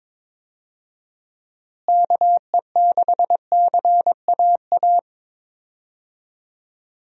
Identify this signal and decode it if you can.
Morse code, heard as a keyed tone, about 22 wpm, 700 Hz. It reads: KE6CAA
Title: USA Call Signs - 6 Characters - Repeated 3x - 22wpm